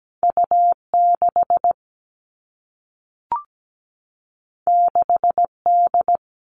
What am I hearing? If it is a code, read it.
Morse code, heard as a keyed tone, about 17 wpm, 700 Hz. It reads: U6 6D
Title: All Cut Numbers - 2-Digit 17wpm